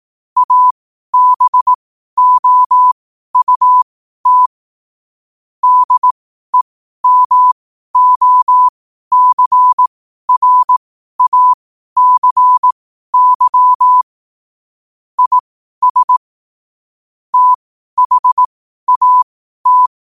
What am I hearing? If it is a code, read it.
Morse code, heard as a keyed tone, about 18 wpm, spaced as for 12 wpm, 1000 Hz. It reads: ABOUT DEMOCRACY IS THAT
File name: qotdfiles / 12wpm